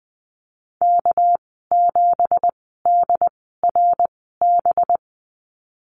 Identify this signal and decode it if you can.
Morse code, heard as a keyed tone, about 19 wpm, spaced as for 14 wpm, 700 Hz. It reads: K7DRB
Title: 231 Lesson 37g - Letter x - Call Signs 20wpm